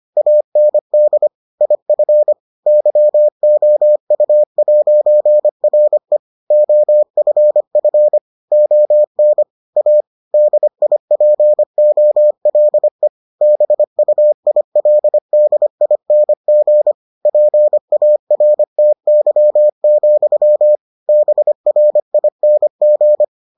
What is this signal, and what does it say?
Morse code, heard as a keyed tone, about 25 wpm, 600 Hz. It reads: AND IF YOU'RE OFF ON A DIPOLE BUILDING PARTY, BRING